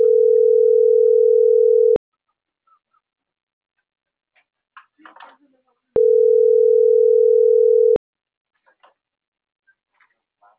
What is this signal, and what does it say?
Ringback tone (North America), 440 + 480 Hz.